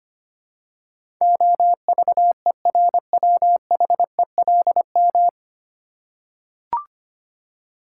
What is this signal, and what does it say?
Morse code, heard as a keyed tone, about 25 wpm, 700 Hz. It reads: OVERWHELM
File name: Nine Letter Words - 25wpm